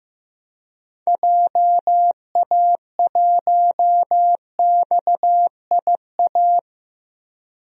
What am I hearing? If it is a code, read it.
Morse code, heard as a keyed tone, about 15 wpm, 700 Hz. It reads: JA1XIA
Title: DXCC-100 Call Signs - 6 Characters - Repeated 2x - 15wpm